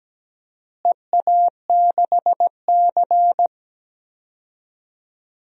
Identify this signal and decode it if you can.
Morse code, heard as a keyed tone, about 17 wpm, 700 Hz. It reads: EA6C
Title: DXCC-100 Call Signs - 4 Characters - Repeated 2x - 17wpm